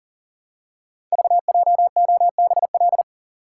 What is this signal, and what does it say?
Morse code, heard as a keyed tone, about 41 wpm, 690 Hz. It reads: VJOBL
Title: Code Groups - 5 Letters - 40wpm